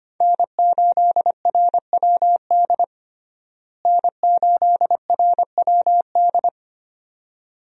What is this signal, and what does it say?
Morse code, heard as a keyed tone, about 25 wpm, 700 Hz. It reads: N8RWD N8RWD
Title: USA Call Signs - 5 Characters - Repeated 2x - 25wpm